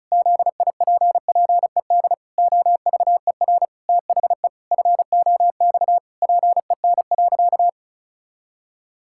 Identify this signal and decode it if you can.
Morse code, heard as a keyed tone, about 36 wpm, 700 Hz. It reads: ZIPPED OVER THE FOX PEN.
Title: Pangram Challenge v2 - 35wpm